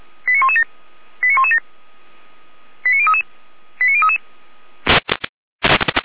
Five-tone selective call (ZVEI), heard as quick runs of five tones, tones of about 70 ms. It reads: 89117 89117 89033 89033